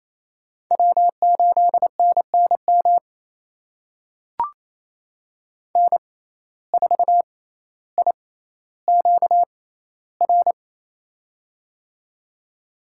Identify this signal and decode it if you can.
Morse code, heard as a keyed tone, about 28 wpm, spaced as for 14 wpm, 700 Hz. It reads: W8NNM N4IQR
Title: USA Call Signs - 5 Characters - 6x Character Spacing - 28wpm